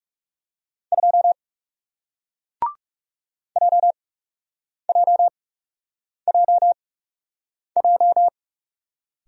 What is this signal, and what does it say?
Morse code, heard as a keyed tone, about 39 wpm, spaced as for 8 wpm, 700 Hz. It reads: 2 JJJJ